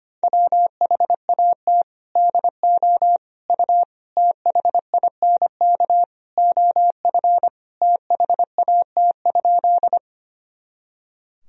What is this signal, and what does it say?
Morse code, heard as a keyed tone, about 25 wpm, 700 Hz. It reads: WHAT DO U THINK OF THAT?
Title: Sentences from Top 200 Words and Abbreviations 25wpm